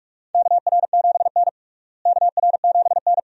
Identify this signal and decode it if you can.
Morse code, heard as a keyed tone, about 45 wpm, 700 Hz. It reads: KR7N KR7N
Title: USA Call Signs - 4 Characters - Repeated 2x - 45wpm